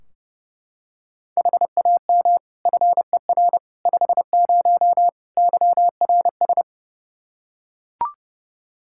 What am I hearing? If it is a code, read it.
Morse code, heard as a keyed tone, about 30 wpm, 700 Hz. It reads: HAM FER 50 YRS
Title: Practice QSO - Exchange 3 - More Info - 30wpm